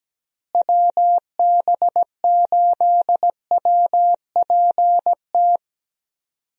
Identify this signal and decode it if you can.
Morse code, heard as a keyed tone, about 17 wpm, 700 Hz. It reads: WB8WPT
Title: USA Call Signs - 6 Characters - Repeated 3x - 17wpm